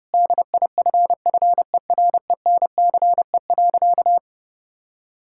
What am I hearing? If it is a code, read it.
Morse code, heard as a keyed tone, about 32 wpm, 700 Hz. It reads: DIFFERENCE.
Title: Sentences from Top 1000 Words - Review Part 5 - 30wpm